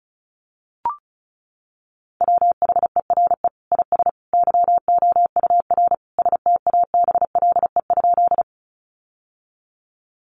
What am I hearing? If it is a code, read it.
Morse code, heard as a keyed tone, about 34 wpm, 700 Hz. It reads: WHERE IS YOUR STABLE?
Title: Sentences from Top 1500 Words 35wpm